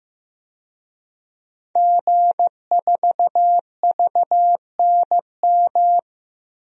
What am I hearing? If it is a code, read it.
Morse code, heard as a keyed tone, about 15 wpm, 700 Hz. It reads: G4VNM
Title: DXCC-100 Call Signs - 5 Characters - 15wpm